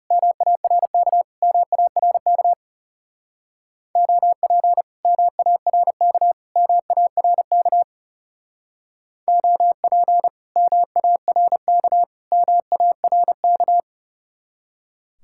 Morse code, heard as a keyed tone, about 36 wpm, 700 Hz. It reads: MARK MARK OP MARK MARK OP MARK MARK
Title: Speed-Racing Common QSO Phrases Encore - 30wpm